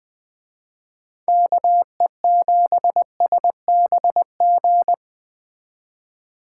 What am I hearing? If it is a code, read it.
Morse code, heard as a keyed tone, about 20 wpm, 700 Hz. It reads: KE7SBG